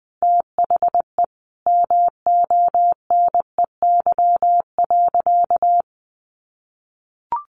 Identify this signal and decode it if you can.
Morse code, heard as a keyed tone, about 20 wpm, 700 Hz. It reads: THE MONEY.
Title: Sentences from Top 1100 Words 20wpm